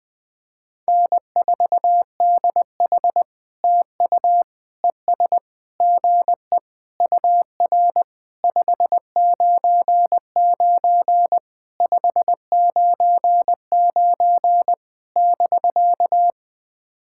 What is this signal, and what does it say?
Morse code, heard as a keyed tone, about 20 wpm, 700 Hz. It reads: N4DH TU ES GE UR 599 599 <BK>
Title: POTA QSO Practice - 20wpm